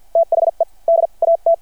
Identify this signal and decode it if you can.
Morse code, heard as a keyed tone, about 51 wpm, spaced as for 40 wpm, 670 Hz. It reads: THE DAT
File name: archived / CRSnet